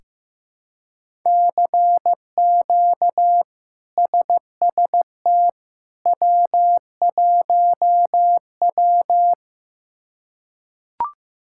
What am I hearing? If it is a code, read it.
Morse code, heard as a keyed tone, about 15 wpm, 700 Hz. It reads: CQ SST W1W
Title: K1USN SST QSO Practice - 15wpm